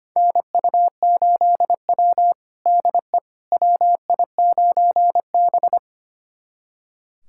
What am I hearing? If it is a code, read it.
Morse code, heard as a keyed tone, about 25 wpm, 700 Hz. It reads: NU8W DE WI9B
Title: USA Callsign DE Callsign - 4 Characters - 25wpm